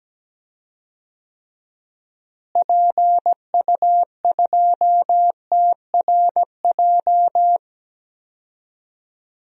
Morse code, heard as a keyed tone, about 17 wpm, 700 Hz. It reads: PU2TRJ
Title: DXCC-100 Call Signs - 6 Characters - Repeated 3x - 17wpm